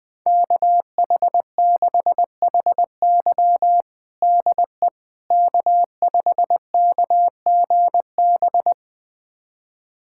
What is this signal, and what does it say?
Morse code, heard as a keyed tone, about 20 wpm, 700 Hz. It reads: KH6HY DE K5KGB
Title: USA Callsign DE Callsign - 5 Characters - 20wpm